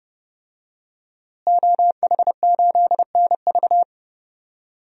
Morse code, heard as a keyed tone, about 30 wpm, 700 Hz. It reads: OH8NV
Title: DXCC-100 Call Signs - 5 Characters - Repeated 3x - 30wpm